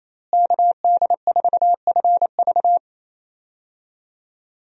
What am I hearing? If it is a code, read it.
Morse code, heard as a keyed tone, about 28 wpm, 700 Hz. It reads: KD4FV
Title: USA Call Signs - 5 Characters - Repeated 2x - 28wpm